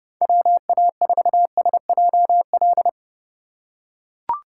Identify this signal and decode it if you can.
Morse code, heard as a keyed tone, about 32 wpm, 700 Hz. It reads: WA4SJL